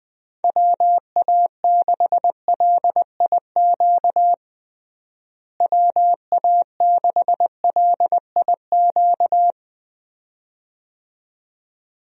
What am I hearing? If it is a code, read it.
Morse code, heard as a keyed tone, about 20 wpm, 700 Hz. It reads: WA6LIQ WA6LIQ
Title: USA Call Signs - 6 Characters - Repeated 2x - 20wpm